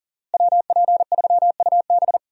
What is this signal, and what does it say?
Morse code, heard as a keyed tone, about 38 wpm, 700 Hz. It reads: WP3UB
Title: USA Call Signs - 5 Characters - Repeated 2x - 40wpm